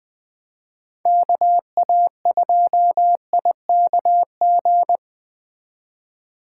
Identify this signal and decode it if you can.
Morse code, heard as a keyed tone, about 20 wpm, 700 Hz. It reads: KA2IKG
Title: Speed Racing - USA 2x3 Call Signs 17wpm